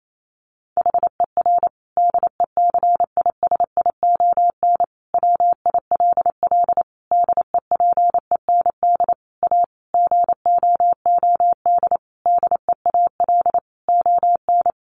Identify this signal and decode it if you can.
Morse code, heard as a keyed tone, about 28 wpm, 700 Hz. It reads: HER DECISION WILL DEPEND A GOOD DEAL ON
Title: Sentences from Top 1300 Words 28wpm